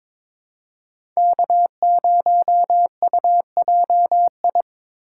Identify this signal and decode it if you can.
Morse code, heard as a keyed tone, about 22 wpm, 700 Hz. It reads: K0UJI
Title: USA Call Signs - 5 Characters - 3x Character Spacing - 22wpm